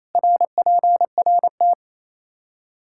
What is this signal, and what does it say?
Morse code, heard as a keyed tone, about 28 wpm, 700 Hz. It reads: RPRT
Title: Practice QSO - Exchange 2 - Info - 28wpm